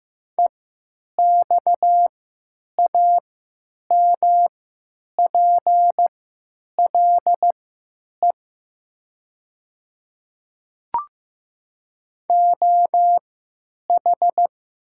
Morse code, heard as a keyed tone, about 15 wpm, spaced as for 9 wpm, 700 Hz. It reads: EXAMPLE OH
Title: 3x Character Spacing - Top 500 Words - 15wpm